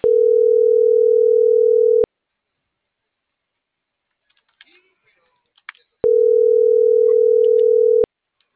Ringback tone (North America), 440 + 480 Hz.